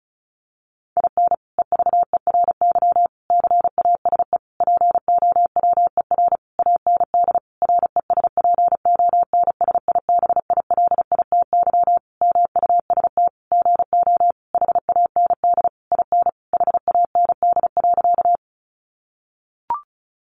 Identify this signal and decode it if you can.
Morse code, heard as a keyed tone, about 34 wpm, 700 Hz. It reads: IN EVERY CASE POWER AND RESPONSIBILITY MUST GO HAND IN HAND.
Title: Sentences from Top 1200 Words 35wpm